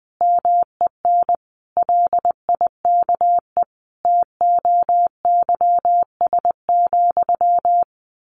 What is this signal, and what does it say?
Morse code, heard as a keyed tone, about 20 wpm, 700 Hz. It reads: MEN LIKE TOYS,
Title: A Sound Of Thunder 20wpm